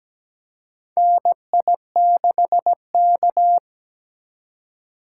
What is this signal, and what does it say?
Morse code, heard as a keyed tone, about 17 wpm, 700 Hz. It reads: NI6K
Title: USA Call Signs - 4 Characters - 4x Character Spacing - 17wpm